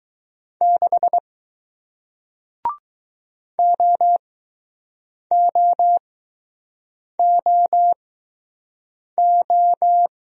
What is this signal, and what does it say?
Morse code, heard as a keyed tone, about 20 wpm, spaced as for 6 wpm, 700 Hz. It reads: 6 OOOO